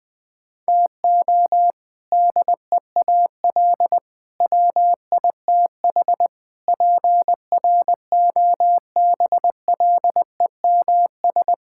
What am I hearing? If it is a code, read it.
Morse code, heard as a keyed tone, about 20 wpm, 700 Hz. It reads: TO DEAL WITH PROBLEMS
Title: Sentences with 3-Word Phrases - 20wpm